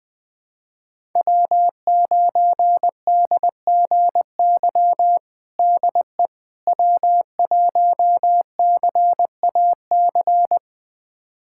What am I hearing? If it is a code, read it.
Morse code, heard as a keyed tone, about 20 wpm, 700 Hz. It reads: W9DGY DE W1CAC
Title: USA Callsign DE Callsign - 5 Characters - 20wpm